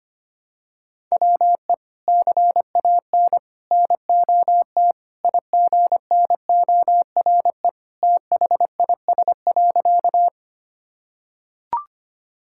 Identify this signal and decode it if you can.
Morse code, heard as a keyed tone, about 25 wpm, 700 Hz. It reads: WE CAN NOT IGNORE THIS.
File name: practice / 25wpm / Sentences with 3-Word Phrases - 25wpm